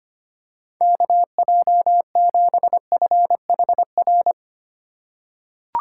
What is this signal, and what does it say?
Morse code, heard as a keyed tone, about 25 wpm, 700 Hz. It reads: KJ7FHR
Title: USA Call Signs - 6 Characters - 2x Character Spacing - 25wpm